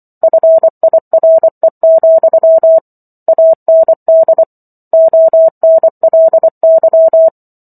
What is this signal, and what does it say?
Morse code, heard as a keyed tone, about 24 wpm, 650 Hz. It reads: FIRE, AND ONLY
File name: Chapter 15 - 24 WPM